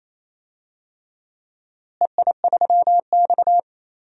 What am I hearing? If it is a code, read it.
Morse code, heard as a keyed tone, about 27 wpm, 700 Hz. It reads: EI3X
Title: DXCC-100 Call Signs - 4 Characters - Repeated 3x - 28wpm